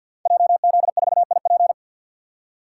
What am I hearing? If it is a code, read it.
Morse code, heard as a keyed tone, about 47 wpm, 700 Hz. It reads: JZ4IP
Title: Code Groups - 5 Letter:Numbers - 50wpm